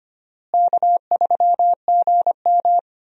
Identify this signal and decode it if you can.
Morse code, heard as a keyed tone, about 25 wpm, 700 Hz. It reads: K3GM
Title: USA Call Signs - 4 Characters - Repeated 3x - 25wpm